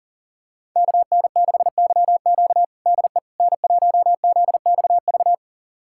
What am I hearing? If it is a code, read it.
Morse code, heard as a keyed tone, about 40 wpm, 700 Hz. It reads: KN6YQ DE N1ZXV
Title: USA Callsign DE Callsign - 5 Characters - 40wpm